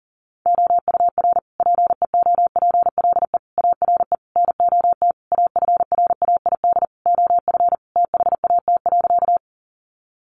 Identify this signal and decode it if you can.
Morse code, heard as a keyed tone, about 40 wpm, 700 Hz. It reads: OUR PEOPLE ARE NOT AFRAID OF THAT.
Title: Sentences from Top 1500 Words 40wpm